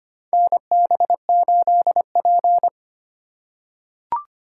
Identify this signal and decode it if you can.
Morse code, heard as a keyed tone, about 25 wpm, 700 Hz. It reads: NB8P
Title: USA Call Signs - 4 Characters - 3x Character Spacing - 25wpm